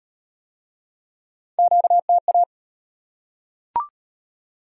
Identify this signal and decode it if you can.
Morse code, heard as a keyed tone, about 38 wpm, 700 Hz. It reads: QTA